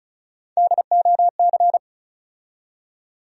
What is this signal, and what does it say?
Morse code, heard as a keyed tone, about 36 wpm, 700 Hz. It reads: DOC
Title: USA Call Sign Postfixes - 3 Letters 35wpm